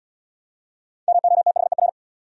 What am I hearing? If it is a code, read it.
Morse code, heard as a keyed tone, about 71 wpm, 700 Hz. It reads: MOTHER